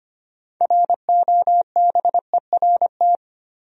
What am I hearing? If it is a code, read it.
Morse code, heard as a keyed tone, about 25 wpm, 700 Hz. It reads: ROBERT
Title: USA Common Names - 25wpm